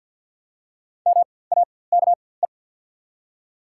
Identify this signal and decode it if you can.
Morse code, heard as a keyed tone, about 47 wpm, spaced as for 23 wpm, 700 Hz. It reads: MAKE
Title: 4x Character Spacing - Top 100 Words - 50wpm